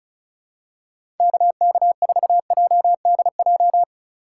Morse code, heard as a keyed tone, about 36 wpm, 700 Hz. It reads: KK4JDJ
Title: USA Call Signs - 6 Characters - 2x Character Spacing - 35wpm